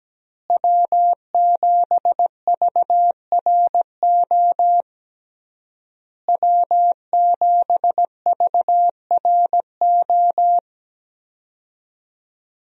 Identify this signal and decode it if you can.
Morse code, heard as a keyed tone, about 17 wpm, 700 Hz. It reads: W7VRO W7VRO
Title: USA Call Signs - 5 Characters - Repeated 3x - 17wpm